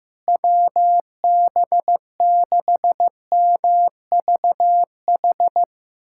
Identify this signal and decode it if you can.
Morse code, heard as a keyed tone, about 15 wpm, 700 Hz. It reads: WB6MVH
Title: USA Callsign DE Callsign - 6 Characters - 15wpm